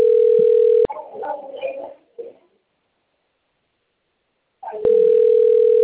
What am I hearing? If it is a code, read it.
Ringback tone (North America), 440 + 480 Hz.